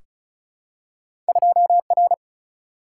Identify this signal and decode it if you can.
Morse code, heard as a keyed tone, about 36 wpm, 700 Hz. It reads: 2R
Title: Single-Letter Single-Number 35wpm